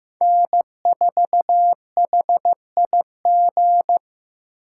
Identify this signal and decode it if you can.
Morse code, heard as a keyed tone, about 18 wpm, 700 Hz. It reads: N4HIG
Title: Speed Racing - USA Call Signs 15wpm